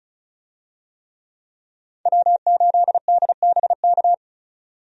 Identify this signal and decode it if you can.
Morse code, heard as a keyed tone, about 36 wpm, 700 Hz. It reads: W8DBK
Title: USA Call Signs - 5 Characters - 35wpm